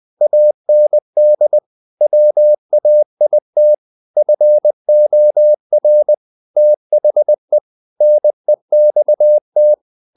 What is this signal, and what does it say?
Morse code, heard as a keyed tone, about 20 wpm, 600 Hz. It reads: AND WAIT FOR THE NEXT
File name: the old man and the sea part 2 20wpm